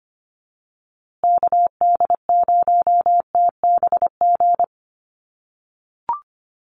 Morse code, heard as a keyed tone, about 25 wpm, 700 Hz. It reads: KD0TBG
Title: USA Call Signs Repeated 3x 25wpm